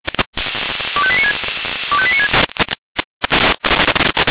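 Five-tone selective call (ZVEI), heard as quick runs of five tones, tones of about 70 ms. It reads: 25896 25896
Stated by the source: ZVEI five-tone selective call received by SDR.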